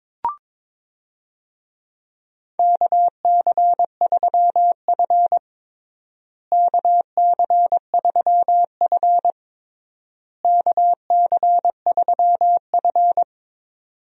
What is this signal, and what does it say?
Morse code, heard as a keyed tone, about 22 wpm, 700 Hz. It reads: KC3F KC3F KC3F
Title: USA Call Signs - 4 Characters - Repeated 3x - 22wpm